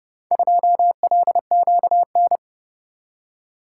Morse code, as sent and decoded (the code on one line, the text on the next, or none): ..--- .-.. --.- -.
2LQN